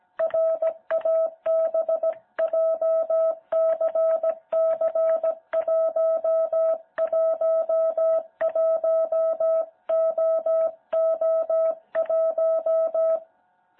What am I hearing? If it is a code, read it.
Morse code, heard as a keyed tone, about 17 wpm, 660 Hz. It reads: RAB JCC111OO1
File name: jh8cbh / 2021cw